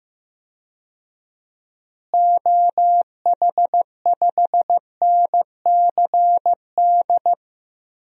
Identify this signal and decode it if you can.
Morse code, heard as a keyed tone, about 16 wpm, 700 Hz. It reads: OH5NCD
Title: DXCC-100 Call Signs - 6 Characters - Repeated 3x - 15wpm